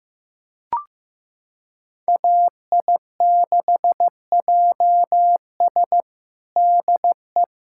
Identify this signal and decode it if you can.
Morse code, heard as a keyed tone, about 15 wpm, 700 Hz. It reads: AI6JS DE
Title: USA Callsign DE Callsign 15wpm